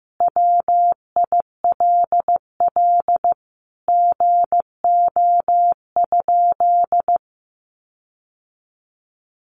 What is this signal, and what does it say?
Morse code, heard as a keyed tone, about 14 wpm, 700 Hz. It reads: WILL GO?
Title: Sentences from Top 1300 Words 15wpm